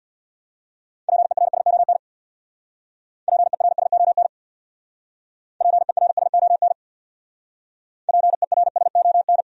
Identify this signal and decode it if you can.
Morse code, heard as a keyed tone, about 64 wpm, 700 Hz. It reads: PERSON PERSON PERSON PERSON